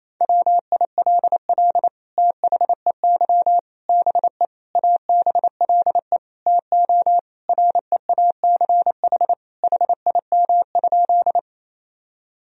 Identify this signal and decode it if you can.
Morse code, heard as a keyed tone, about 27 wpm, 700 Hz. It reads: WILL THEY BE ABLE TO REACH HIM?
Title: Sentences from Top 1000 Words - Review Part 2 - 28wpm